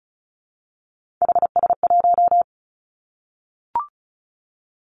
Morse code, heard as a keyed tone, about 36 wpm, 700 Hz. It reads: HS1
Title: Three Challenging Letter-Number Combinations 35wpm